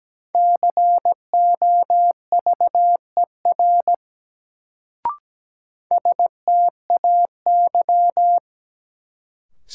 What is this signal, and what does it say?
Morse code, heard as a keyed tone, about 17 wpm, 700 Hz. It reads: COVER STAY
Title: Sentences from Top 500 Words - New Word Drill - 17wpm